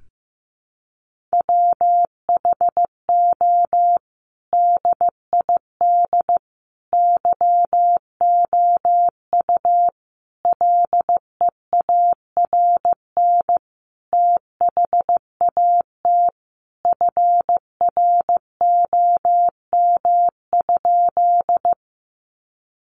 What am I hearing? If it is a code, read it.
Morse code, heard as a keyed tone, about 15 wpm, 700 Hz. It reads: WHO DID YOU LEARN THAT FROM?
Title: Sentences from Top 300 Words 15wpm